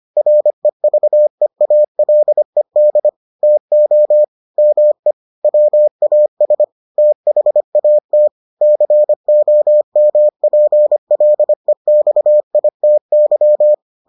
Morse code, heard as a keyed tone, about 25 wpm, 600 Hz. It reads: REVEALED TO ME WAS THAT COMPLEXITY